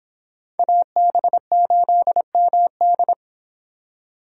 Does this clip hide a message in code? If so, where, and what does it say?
Yes: Morse code from 0.6 to 3.1 s, heard as a keyed tone, about 26 wpm, 700 Hz. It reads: AB8MD